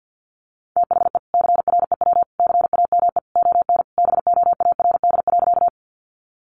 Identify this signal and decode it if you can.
Morse code, heard as a keyed tone, about 50 wpm, 700 Hz. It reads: THE CREW CAME ON BOARD.
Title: Sentences from Top 1500 Words 50wpm